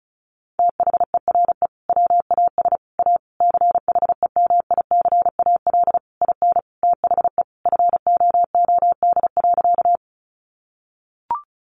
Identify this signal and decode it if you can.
Morse code, heard as a keyed tone, about 34 wpm, 700 Hz. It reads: THERE WAS A CHEMICAL IN THE FOOD.
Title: Sentences from Top 1200 Words 35wpm